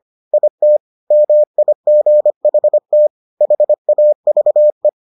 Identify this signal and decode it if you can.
Morse code, heard as a keyed tone, about 25 wpm, 600 Hz. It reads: IT MIGHT HAVE